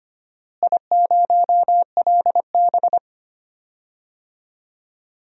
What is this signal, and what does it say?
Morse code, heard as a keyed tone, about 25 wpm, 700 Hz. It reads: I0LB
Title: DXCC-100 Call Signs - 4 Characters - Repeated 2x - 25wpm